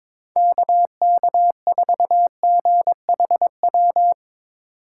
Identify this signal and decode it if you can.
Morse code, heard as a keyed tone, about 22 wpm, 700 Hz. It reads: KK4GHW
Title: USA Call Signs - 6 Characters - 4x Character Spacing - 22wpm